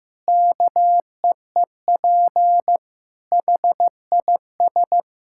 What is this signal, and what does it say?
Morse code, heard as a keyed tone, about 15 wpm, 700 Hz. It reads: KEEP HIS
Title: Sentences from Top 1000 Words - Review Part 6 - 15wpm